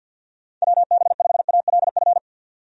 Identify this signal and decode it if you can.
Morse code, heard as a keyed tone, about 49 wpm, 700 Hz. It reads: WB5ALF
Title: USA Call Signs - 6 Characters - 50wpm